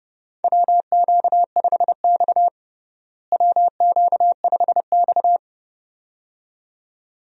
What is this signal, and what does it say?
Morse code, heard as a keyed tone, about 29 wpm, 700 Hz. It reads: WQ5X WQ5X
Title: USA Call Signs - 4 Characters - Repeated 2x - 30wpm